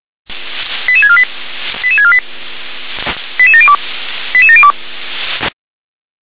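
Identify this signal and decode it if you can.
Five-tone selective call (ZVEI), heard as quick runs of five tones, tones of about 70 ms. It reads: 88648 88648 88702 88702
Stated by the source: ZVEI five-tone call received on SDR.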